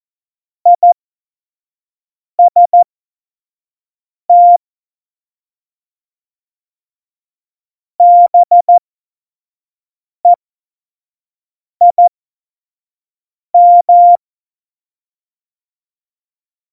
Morse code, heard as a keyed tone, about 14 wpm, spaced as for 5 wpm, 700 Hz. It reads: IST BEIM
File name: Morsekurs / Witze / Witz 03.Juli 2018, 25 F70